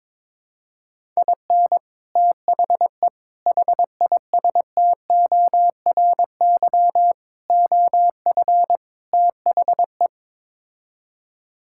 Morse code, heard as a keyed tone, about 22 wpm, 700 Hz. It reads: IN THE HISTORY OF THE
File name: practice / 22wpm / Build A Phrase - 5-Word Phrases - 22wpm